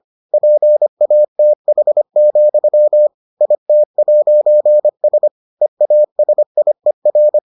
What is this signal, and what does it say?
Morse code, heard as a keyed tone, about 25 wpm, 600 Hz. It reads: PATH, IT'S EASIER